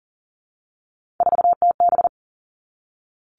Morse code, heard as a keyed tone, about 40 wpm, 700 Hz. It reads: 4TB
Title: All Cut Numbers - 3-Digit 40wpm